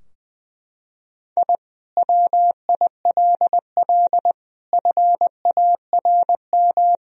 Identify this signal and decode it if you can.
Morse code, heard as a keyed tone, about 20 wpm, 700 Hz. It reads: I WILL FARM
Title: Sentences from Top 1000 Words - Review Part 2 - 20wpm